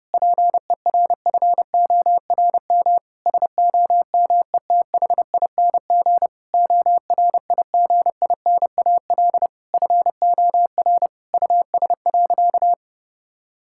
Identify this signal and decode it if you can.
Morse code, heard as a keyed tone, about 32 wpm, 700 Hz. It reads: PERFORM SOMETHING ORIGINAL FOR US.
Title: Sentences from Top 1000 Words - Review Part 6 - 30wpm